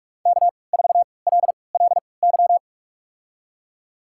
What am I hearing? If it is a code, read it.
Morse code, heard as a keyed tone, about 47 wpm, spaced as for 26 wpm, 700 Hz. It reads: K4LLY
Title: USA Call Signs - 5 Characters - 3x Character Spacing - 45wpm